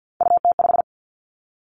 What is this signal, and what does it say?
Morse code, heard as a keyed tone, about 50 wpm, 700 Hz. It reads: UT5
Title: Callsign Prefixes for Top 25 DXCC - Repeated 3x - 50wpm